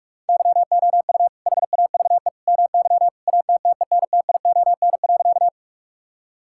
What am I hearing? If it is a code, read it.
Morse code, heard as a keyed tone, about 46 wpm, 700 Hz. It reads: YOU HAVE MY ATTENTION.
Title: Sentences from Top 1000 Words - Review Part 6 - 45wpm